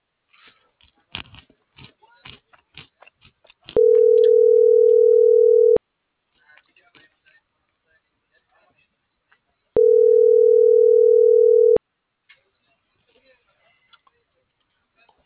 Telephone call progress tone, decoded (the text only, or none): ringback tone